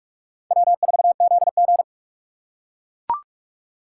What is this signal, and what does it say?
Morse code, heard as a keyed tone, about 43 wpm, 700 Hz. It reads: W4ZG
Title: USA Call Signs - 4 Characters - 8x Character Spacing - 45wpm